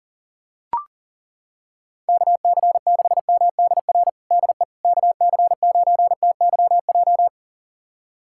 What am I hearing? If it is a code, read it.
Morse code, heard as a keyed tone, about 40 wpm, 700 Hz. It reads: KC6MDR DE KC9TYJ
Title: USA Callsign DE Callsign - 6 Characters - 40wpm